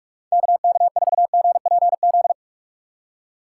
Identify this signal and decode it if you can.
Morse code, heard as a keyed tone, about 45 wpm, 700 Hz. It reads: KK4GPZ
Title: USA Call Signs - 6x Character Spacing - 45wpm